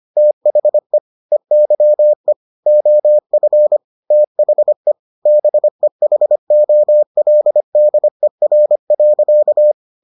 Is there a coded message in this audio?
Yes: Morse code from 0.2 to 9.7 s, heard as a keyed tone, about 25 wpm, 600 Hz. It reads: THE EYE OF THE BEHOLDER.